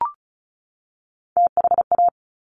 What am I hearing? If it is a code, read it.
Morse code, heard as a keyed tone, about 36 wpm, 700 Hz. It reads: THA
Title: Top Three Letter Combinations in Words 35wpm